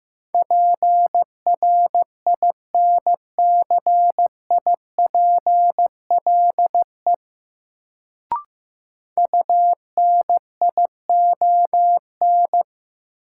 Morse code, heard as a keyed tone, about 15 wpm, 700 Hz. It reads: PRINCIPLE UNION
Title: Sentences from Top 1400 Words - New Word Drill - 15wpm